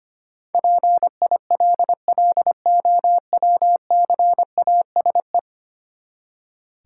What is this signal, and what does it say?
Morse code, heard as a keyed tone, about 25 wpm, 690 Hz. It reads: PILLOWCASE
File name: practice / 25wpm / Easy Words with Hints - 25wpm